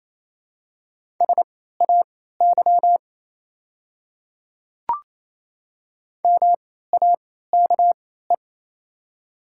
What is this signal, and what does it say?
Morse code, heard as a keyed tone, about 28 wpm, spaced as for 16 wpm, 700 Hz. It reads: SAY MAKE
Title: 3x Character Spacing - Top 200 Words - 28wpm